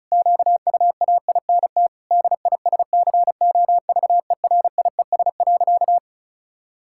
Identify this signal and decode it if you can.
Morse code, heard as a keyed tone, about 36 wpm, 700 Hz. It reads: QUAINT DISCOVERIES.
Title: Pangram Challenge v2 - 35wpm